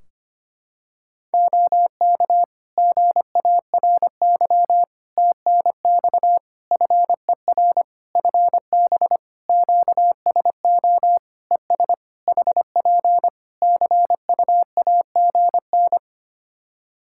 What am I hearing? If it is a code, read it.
Morse code, heard as a keyed tone, about 25 wpm, 700 Hz. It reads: OK GARY TNX FER FB QSO ES HP CUAGN
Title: Practice QSO - Exchange 4 - Ending - 25wpm